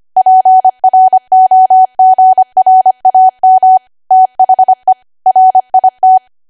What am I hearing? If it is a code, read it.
Morse code, heard as a keyed tone, about 25 wpm, 750 Hz. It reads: PROGRAM THE RIT
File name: Archive / 25 WPM / 25WPM